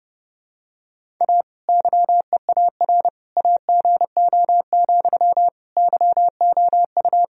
Morse code, heard as a keyed tone, about 30 wpm, 700 Hz. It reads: A YEAR AGO, YOU
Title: Sentences from Top 1000 Words - Review Part 3 - 30wpm